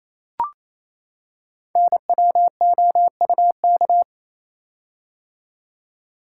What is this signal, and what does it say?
Morse code, heard as a keyed tone, about 27 wpm, 700 Hz. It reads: NWOUK
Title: Code Groups - 5 Letters - 28wpm